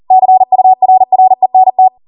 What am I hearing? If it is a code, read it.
Morse code, heard as a keyed tone, about 40 wpm, 750 Hz. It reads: CURRENT